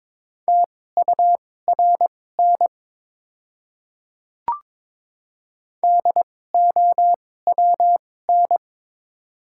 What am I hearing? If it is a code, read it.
Morse code, heard as a keyed tone, about 22 wpm, spaced as for 16 wpm, 690 Hz. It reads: TURN DOWN
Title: 2x Character Spacing - Top 300 Words - 22wpm